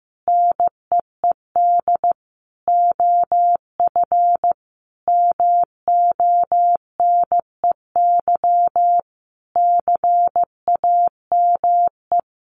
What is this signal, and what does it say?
Morse code, heard as a keyed tone, about 15 wpm, 700 Hz. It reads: NEED OF MONEY CAME
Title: Sentences from Top 1500 Words 15wpm